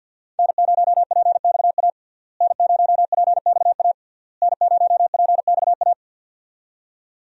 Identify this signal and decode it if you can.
Morse code, heard as a keyed tone, about 51 wpm, 700 Hz. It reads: N0PXA N0PXA N0PXA
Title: USA Call Signs - 5 Characters - Repeated 3x - 50wpm